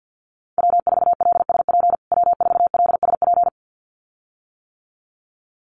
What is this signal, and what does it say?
Morse code, heard as a keyed tone, about 48 wpm, 700 Hz. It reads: W4LSP W4LSP
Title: USA Call Signs Repeated 2x 50wpm